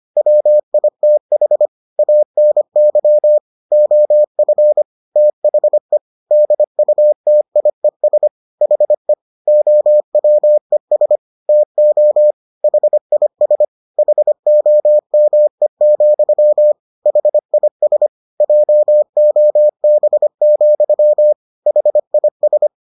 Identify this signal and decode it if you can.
Morse code, heard as a keyed tone, about 25 wpm, 600 Hz. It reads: WITH ANY OF THE DUTIES HE OWES TO HIS HOME, HIS JOB, HIS